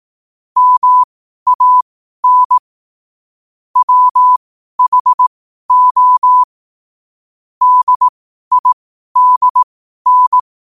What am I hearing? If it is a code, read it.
Morse code, heard as a keyed tone, about 18 wpm, spaced as for 12 wpm, 1000 Hz. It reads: MAN WHO DIDN
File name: qotdfiles / 12wpm